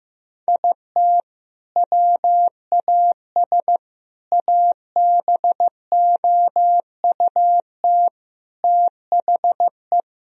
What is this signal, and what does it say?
Morse code, heard as a keyed tone, about 16 wpm, 700 Hz. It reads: IT WAS ABOUT THE
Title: Sentences from Top 1000 Words - Review Part 1 - 15wpm